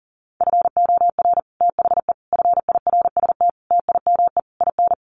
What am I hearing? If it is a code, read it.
Morse code, heard as a keyed tone, about 40 wpm, 700 Hz. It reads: FOR THE FIRST TIME IN
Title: Sets of 5 Words 40wpm